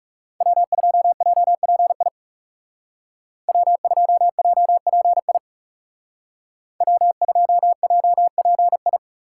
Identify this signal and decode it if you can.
Morse code, heard as a keyed tone, about 35 wpm, 700 Hz. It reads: W2JPI W2JPI W2JPI
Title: Speed Racing - USA 1x3 Call Signs 30wpm